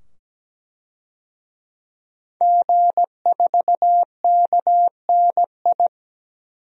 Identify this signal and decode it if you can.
Morse code, heard as a keyed tone, about 17 wpm, 700 Hz. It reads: G4KNI
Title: DXCC-100 Call Signs - 5 Characters - 17wpm